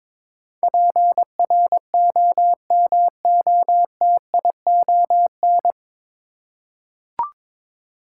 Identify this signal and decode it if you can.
Morse code, heard as a keyed tone, about 22 wpm, 700 Hz. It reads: PROMOTION
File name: 9 Letter Words - First 3 Letters Given - 22wpm